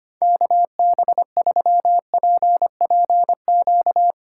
Morse code, heard as a keyed tone, about 25 wpm, 700 Hz. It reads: KB3PPQ
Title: USA Call Signs - 3x Character Spacing - 25wpm